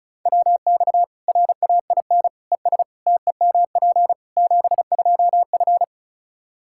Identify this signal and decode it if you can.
Morse code, heard as a keyed tone, about 36 wpm, 700 Hz. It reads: WX RAIN ES TEMP 72F
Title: K1USN SST QSO Practice 35wpm